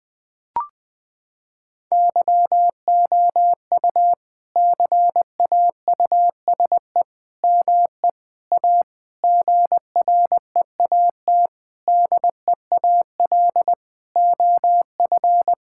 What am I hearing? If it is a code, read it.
Morse code, heard as a keyed tone, about 20 wpm, 700 Hz. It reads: YOU CAUSE ME A GREAT DEAL OF
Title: Sentences from Top 1600 Words 20wpm